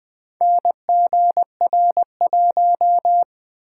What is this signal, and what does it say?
Morse code, heard as a keyed tone, about 20 wpm, 700 Hz. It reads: NGR1
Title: Four Letter-Number Combinations - 20wpm